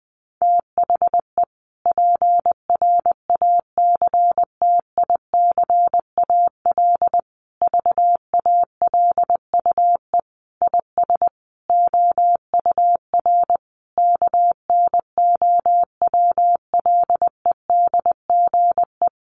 Morse code, heard as a keyed tone, about 20 wpm, 700 Hz. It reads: THE PRACTICAL VALUE IS OUR KNOWLEDGE
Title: Sentences from Top 1400 Words 20wpm